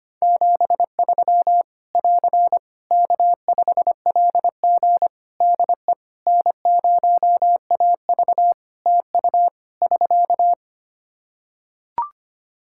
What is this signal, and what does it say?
Morse code, heard as a keyed tone, about 25 wpm, 700 Hz. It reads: 73 <AR> K5LG DE N0AV TU <SK>
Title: K1USN SST QSO Practice 25wpm